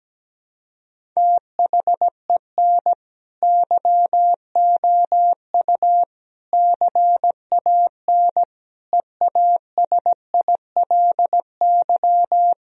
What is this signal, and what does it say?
Morse code, heard as a keyed tone, about 17 wpm, 700 Hz. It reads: THEN YOU CAN EASILY
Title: Sentences from Top 1000 Words - Review Part 1 - 17wpm